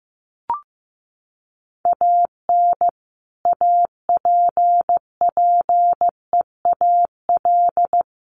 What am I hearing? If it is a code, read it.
Morse code, heard as a keyed tone, about 15 wpm, 700 Hz. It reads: AN APPEAL
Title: Sentences from Top 1300 Words 15wpm